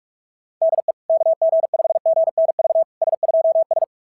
Morse code, heard as a keyed tone, about 40 wpm, 660 Hz. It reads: DE KG5GNV S2S
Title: SOTA QSO Practice 45wpm